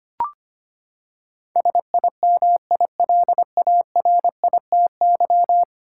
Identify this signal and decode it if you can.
Morse code, heard as a keyed tone, about 25 wpm, 700 Hz. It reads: SIMILARITY
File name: Ten Letter Words - 25wpm